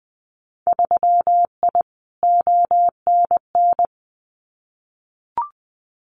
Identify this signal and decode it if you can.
Morse code, heard as a keyed tone, about 20 wpm, 700 Hz. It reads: 3I ONN
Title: Winter Field Day Exchanges Encore 20wpm